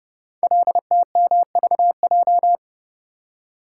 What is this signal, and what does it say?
Morse code, heard as a keyed tone, about 29 wpm, 700 Hz. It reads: LTMVJ
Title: Code Groups - 5 Letters - 30wpm